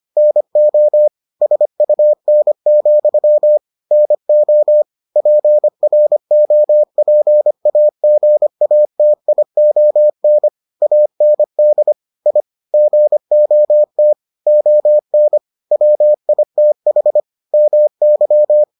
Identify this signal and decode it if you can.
Morse code, heard as a keyed tone, about 25 wpm, 600 Hz. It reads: NO SUN, NO PROPAGATION AND I GOT ON WITH MY